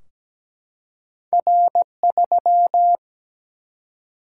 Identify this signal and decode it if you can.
Morse code, heard as a keyed tone, about 17 wpm, 700 Hz. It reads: R3
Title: Single-Letter Single-Number 17wpm